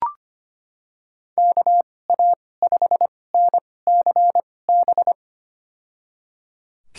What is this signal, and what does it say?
Morse code, heard as a keyed tone, about 25 wpm, spaced as for 18 wpm, 700 Hz. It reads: KA5NCB
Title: USA Call Signs - 6 Characters - 2x Character Spacing - 25wpm